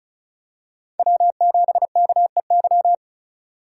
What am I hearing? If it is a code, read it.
Morse code, heard as a keyed tone, about 36 wpm, 700 Hz. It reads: W7KEY
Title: USA Call Signs - 2x Character Spacing - 35wpm